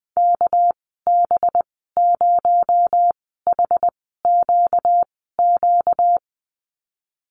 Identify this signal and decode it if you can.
Morse code, heard as a keyed tone, about 20 wpm, 700 Hz. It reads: K B 0 H Q Q
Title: 238 Lesson 38g - Letter q - Call Signs 20wpm